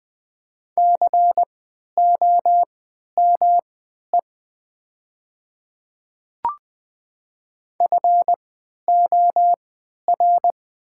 Morse code, heard as a keyed tone, about 20 wpm, spaced as for 11 wpm, 700 Hz. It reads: COME FOR
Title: 3x Character Spacing - Top 100 Words - 20wpm